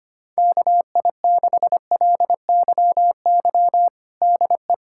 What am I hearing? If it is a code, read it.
Morse code, heard as a keyed tone, about 25 wpm, 700 Hz. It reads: KI6LYY DE
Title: USA Callsign DE Callsign - 6 Characters - 25wpm